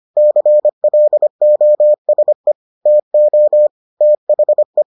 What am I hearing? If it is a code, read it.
Morse code, heard as a keyed tone, about 25 wpm, 600 Hz. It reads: CLOSE TO THE